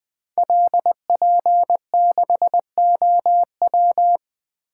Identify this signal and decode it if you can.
Morse code, heard as a keyed tone, about 20 wpm, 700 Hz. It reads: LP6OW
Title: Code Groups - 5 Letter:Numbers - 20wpm